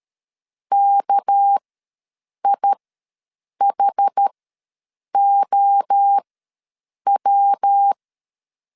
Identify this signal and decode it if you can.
Morse code, heard as a keyed tone, about 13 wpm, spaced as for 7 wpm, 800 Hz. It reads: KIHOW
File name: LEZIONI / 7 GF9Z / 7 GF9 ws 12wpm 3344